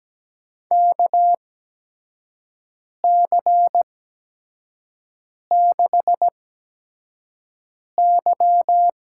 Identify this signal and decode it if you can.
Morse code, heard as a keyed tone, about 17 wpm, spaced as for 5 wpm, 700 Hz. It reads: KC6Y
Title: USA Call Signs - 6 Characters - 8x Character Spacing - 17wpm